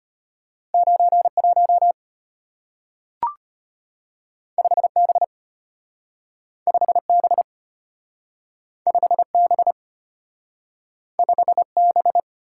Morse code, heard as a keyed tone, about 34 wpm, 700 Hz. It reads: 91 5B 5B 5B 5B